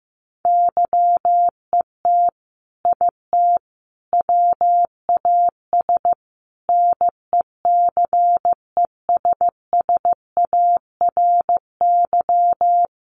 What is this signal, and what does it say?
Morse code, heard as a keyed tone, about 15 wpm, 700 Hz. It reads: YET IT WAS NECESSARY
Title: Sentences from Top 800 Words 15wpm